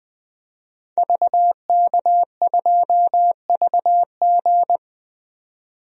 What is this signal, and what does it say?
Morse code, heard as a keyed tone, about 21 wpm, 700 Hz. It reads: VK2VG
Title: DXCC-100 Call Signs - 5 Characters - 20wpm